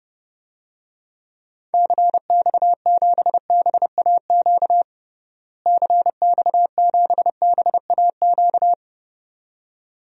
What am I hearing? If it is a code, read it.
Morse code, heard as a keyed tone, about 30 wpm, 700 Hz. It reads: CX7BAQ CX7BAQ
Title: DXCC-100 Call Signs - 6 Characters - Repeated 2x - 30wpm